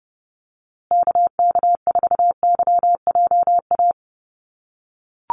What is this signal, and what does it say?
Morse code, heard as a keyed tone, about 30 wpm, 700 Hz. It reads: KK4YJA
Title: USA Call Signs Repeated 2x 30wpm